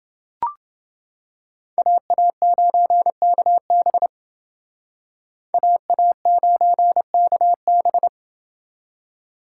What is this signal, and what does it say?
Morse code, heard as a keyed tone, about 28 wpm, 700 Hz. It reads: AA9KB AA9KB